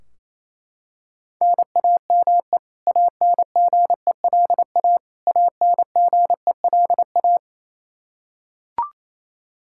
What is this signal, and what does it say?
Morse code, heard as a keyed tone, about 28 wpm, 700 Hz. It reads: NAME ANGELA ANGELA
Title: Practice QSO - Exchange 1 - Beginning - 28wpm